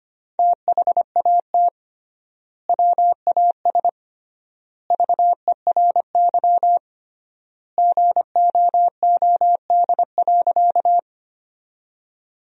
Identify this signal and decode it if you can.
Morse code, heard as a keyed tone, about 25 wpm, 700 Hz. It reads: THAT WAS VERY GOOD.
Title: 3x Word-Spacing - Sentences from Top 200 Words - 25wpm